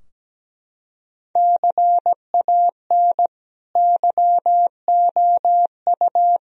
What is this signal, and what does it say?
Morse code, heard as a keyed tone, about 17 wpm, 700 Hz. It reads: CAN YOU
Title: Sentences from Top 1000 Words - Review Part 2 - 17wpm